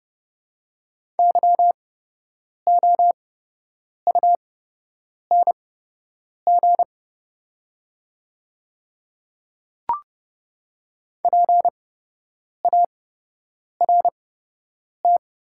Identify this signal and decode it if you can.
Morse code, heard as a keyed tone, about 30 wpm, spaced as for 8 wpm, 700 Hz. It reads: YOUNG PART
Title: 8x Character Spacing - Top 500 Words - 30wpm